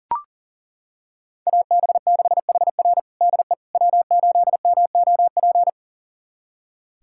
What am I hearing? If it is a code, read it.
Morse code, heard as a keyed tone, about 40 wpm, 700 Hz. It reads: AB6HR DE W8MOP
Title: USA Callsign DE Callsign - 5 Characters - 40wpm